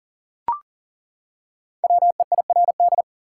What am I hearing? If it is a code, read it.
Morse code, heard as a keyed tone, about 40 wpm, 700 Hz. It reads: WEIRD
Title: Five Letter Words - 40wpm